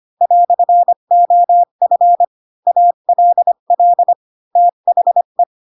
Morse code, heard as a keyed tone, about 25 wpm, 700 Hz. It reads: "OF ALL THE
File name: Pinocchio / Pinocchio ch4 20 25 ik1hge